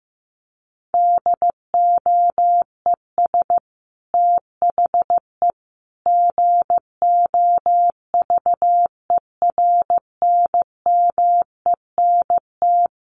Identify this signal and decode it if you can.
Morse code, heard as a keyed tone, about 15 wpm, 700 Hz. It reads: DOES THE GOVERNMENT
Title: Sentences from Top 1500 Words 15wpm